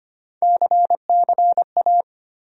Morse code, heard as a keyed tone, about 25 wpm, 700 Hz. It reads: CCA
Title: USA Call Sign Postfixes - 3 Letters 25wpm